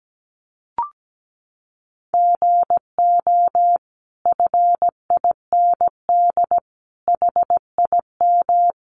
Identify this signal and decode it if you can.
Morse code, heard as a keyed tone, about 17 wpm, 700 Hz. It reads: GO FIND HIM
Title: Sentences from Top 1100 Words 17wpm